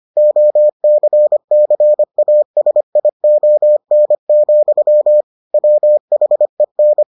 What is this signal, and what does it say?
Morse code, heard as a keyed tone, about 25 wpm, 600 Hz. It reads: OCCASION, WHEN